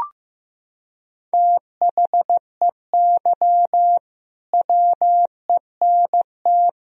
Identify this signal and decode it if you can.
Morse code, heard as a keyed tone, about 15 wpm, 700 Hz. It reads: THEY WENT
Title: Sentences from Top 1000 Words - Review Part 1 - 15wpm